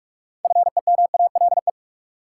Morse code, heard as a keyed tone, about 46 wpm, 700 Hz. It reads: FEMALE
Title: Six Letter Words - 45wpm